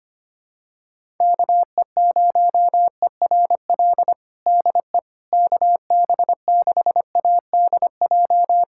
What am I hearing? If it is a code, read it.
Morse code, heard as a keyed tone, about 25 wpm, 700 Hz. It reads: KE0ERL DE KB6ADJ
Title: USA Callsign DE Callsign 25wpm